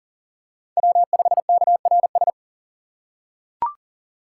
Morse code, heard as a keyed tone, about 39 wpm, 700 Hz. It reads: W5KRS
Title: USA Call Signs - 5 Characters - 2x Character Spacing - 40wpm